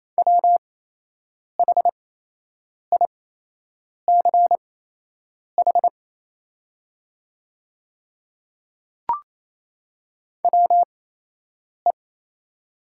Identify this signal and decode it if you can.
Morse code, heard as a keyed tone, about 28 wpm, spaced as for 8 wpm, 700 Hz. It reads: WHICH WE
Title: 8x Character Spacing - Top 100 Words - 28wpm